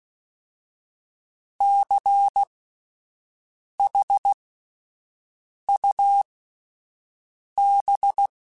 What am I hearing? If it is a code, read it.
Morse code, heard as a keyed tone, about 16 wpm, spaced as for 5 wpm, 800 Hz. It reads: CHUB
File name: Alphabetical / Alphabetical mp3s / Cs @5